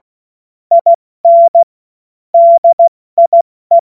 Morse code, heard as a keyed tone, about 16 wpm, 680 Hz. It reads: IN DIE
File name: Morsekurs / Witze / Witz 08. November 2022 70 F80